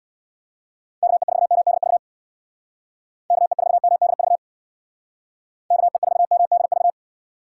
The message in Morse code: KE4MDV KE4MDV KE4MDV

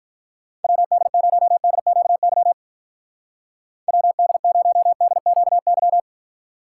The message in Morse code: WD0DQY WD0DQY

